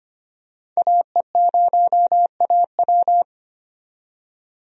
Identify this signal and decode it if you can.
Morse code, heard as a keyed tone, about 25 wpm, 700 Hz. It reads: AE0AW
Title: USA Call Signs - 5 Characters - 25wpm